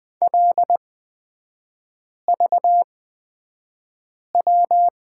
Morse code, heard as a keyed tone, about 20 wpm, spaced as for 5 wpm, 700 Hz. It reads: LVW
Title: 243 Lesson 39e - Letter z - Three Characters 20wpm